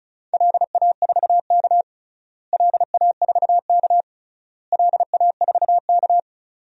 Morse code, heard as a keyed tone, about 34 wpm, 700 Hz. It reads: LA4K LA4K LA4K
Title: DXCC-100 Call Signs - 4 Characters - Repeated 3x - 35wpm